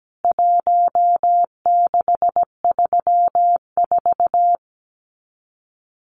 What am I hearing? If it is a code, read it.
Morse code, heard as a keyed tone, about 17 wpm, 700 Hz. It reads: 1634
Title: CWT Exchange 17wpm